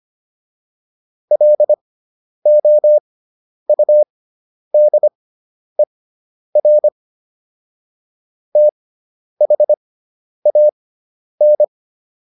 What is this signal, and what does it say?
Morse code, heard as a keyed tone, about 25 wpm, spaced as for 10 wpm, 600 Hz. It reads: LOUDER THAN